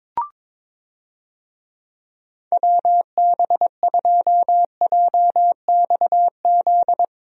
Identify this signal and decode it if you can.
Morse code, heard as a keyed tone, about 22 wpm, 700 Hz. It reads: WB2JXZ
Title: USA Call Signs - 6 Characters - 22wpm